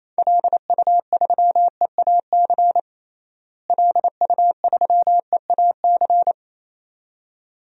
Morse code, heard as a keyed tone, about 28 wpm, 700 Hz. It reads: LU3EAC LU3EAC
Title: DXCC-100 Call Signs - 6 Characters - Repeated 2x - 28wpm